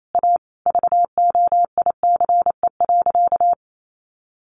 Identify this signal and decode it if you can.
Morse code, heard as a keyed tone, about 28 wpm, 700 Hz. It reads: A VOICE.
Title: Sentences from Top 500 Words 28wpm